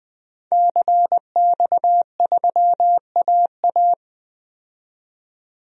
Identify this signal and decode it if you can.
Morse code, heard as a keyed tone, about 20 wpm, 700 Hz. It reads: CX3AA
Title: DXCC-100 Call Signs - 5 Characters - 20wpm